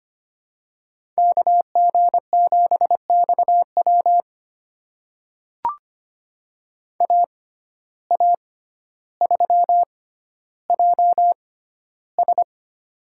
Morse code, heard as a keyed tone, about 25 wpm, spaced as for 13 wpm, 700 Hz. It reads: KG7XW AA3JS
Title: USA Call Signs - 5 Characters - 6x Character Spacing - 25wpm